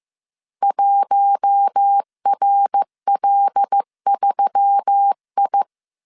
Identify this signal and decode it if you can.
Morse code, heard as a keyed tone, about 15 wpm, 800 Hz. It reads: 1RL3I
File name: CW / DET / 1 6 1500C S A 14wpm 2921